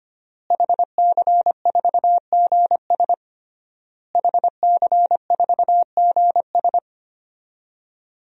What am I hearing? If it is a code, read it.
Morse code, heard as a keyed tone, about 25 wpm, 700 Hz. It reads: HC4GS HC4GS
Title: DXCC-100 Call Signs - 5 Characters - Repeated 3x - 25wpm